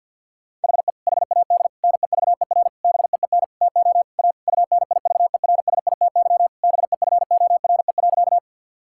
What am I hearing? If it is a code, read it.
Morse code, heard as a keyed tone, about 52 wpm, 700 Hz. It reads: HE HAD NEVER BEEN TO A UNIVERSITY BEFORE.
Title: Sentences from Top 1000 Words - Review Part 5 - 50wpm